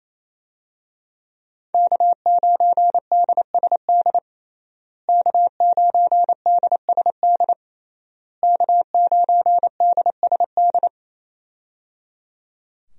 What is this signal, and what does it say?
Morse code, heard as a keyed tone, about 27 wpm, 700 Hz. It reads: K9DSD K9DSD K9DSD
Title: USA Call Signs - 5 Characters - Repeated 3x - 28wpm